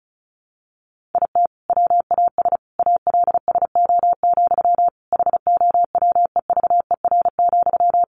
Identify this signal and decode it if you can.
Morse code, heard as a keyed tone, about 34 wpm, 700 Hz. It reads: IT WAS ALSO, HOWEVER,
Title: Sentences from Top 1400 Words 35wpm